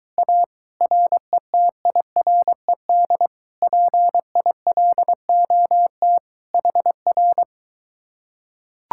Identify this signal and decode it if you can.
Morse code, heard as a keyed tone, about 23 wpm, 700 Hz. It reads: A RETIRED PILOT HR